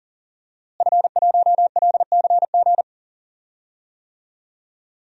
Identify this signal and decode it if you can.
Morse code, heard as a keyed tone, about 40 wpm, 700 Hz. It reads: F1LCG
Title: DXCC-100 Call Signs - 5 Characters - Repeated 3x - 40wpm